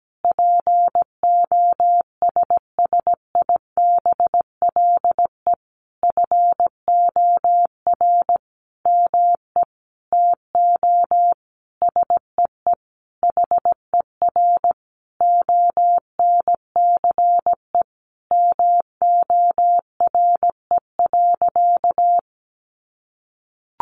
Morse code, heard as a keyed tone, about 17 wpm, 700 Hz. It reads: POSSIBLE FOR ME TO SEE HER ONCE MORE.
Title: Sentences from Top 500 Words 17wpm